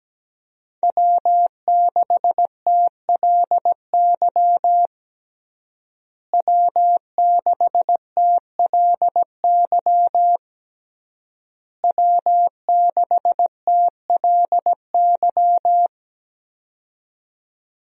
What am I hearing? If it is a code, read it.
Morse code, heard as a keyed tone, about 17 wpm, 700 Hz. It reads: W6TLY W6TLY W6TLY
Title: USA Call Signs - 5 Characters - Repeated 3x - 17wpm